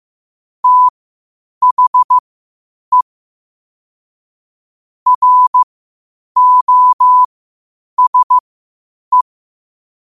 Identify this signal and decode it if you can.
Morse code, heard as a keyed tone, about 15 wpm, spaced as for 8 wpm, 1000 Hz. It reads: THE ROSE
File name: qotdfiles / 8wpm